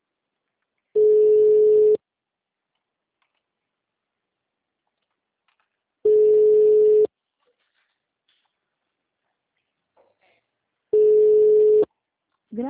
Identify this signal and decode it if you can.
Ringback tone (Europe, 425 Hz), with speech.
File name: out- -2027- - - .91025